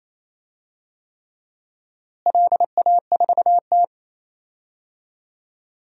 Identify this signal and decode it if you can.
Morse code, heard as a keyed tone, about 28 wpm, 700 Hz. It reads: LA4T
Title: DXCC-100 Call Signs - 4 Characters - Repeated 3x - 28wpm